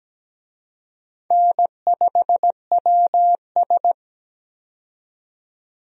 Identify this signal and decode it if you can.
Morse code, heard as a keyed tone, about 17 wpm, 700 Hz. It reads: N5WS
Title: USA Call Signs - 4 Characters - 17wpm